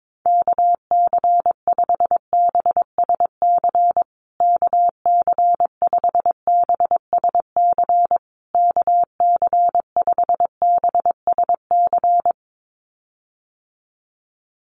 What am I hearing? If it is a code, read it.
Morse code, heard as a keyed tone, about 22 wpm, 700 Hz. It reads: KC5BSC KC5BSC KC5BSC
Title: USA Call Signs Repeated 3x 22wpm